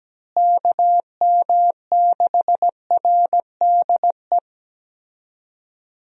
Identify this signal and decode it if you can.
Morse code, heard as a keyed tone, about 17 wpm, 700 Hz. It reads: KM6RDE
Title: USA Call Signs - 6 Characters - Repeated 2x - 17wpm